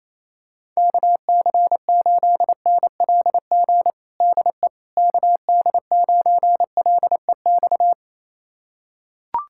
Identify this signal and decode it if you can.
Morse code, heard as a keyed tone, about 28 wpm, 700 Hz. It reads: KC8NLG DE KD9LEX
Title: USA Callsign DE Callsign - 6 Characters - 28wpm